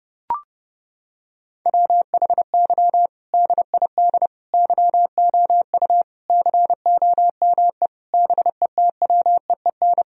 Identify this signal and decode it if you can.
Morse code, heard as a keyed tone, about 30 wpm, 700 Hz. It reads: WHY DID YOU COME BETWEEN
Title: Sentences from Top 1000 Words - Review Part 1 - 30wpm